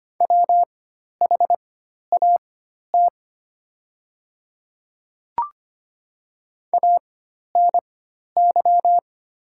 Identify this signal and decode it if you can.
Morse code, heard as a keyed tone, about 25 wpm, spaced as for 12 wpm, 690 Hz. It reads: WHAT ANY
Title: 4x Character Spacing - Top 200 Words - 25wpm